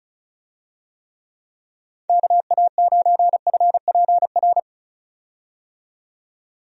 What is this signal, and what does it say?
Morse code, heard as a keyed tone, about 36 wpm, 700 Hz. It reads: KA9FPR
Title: USA Call Signs - 6 Characters - 35wpm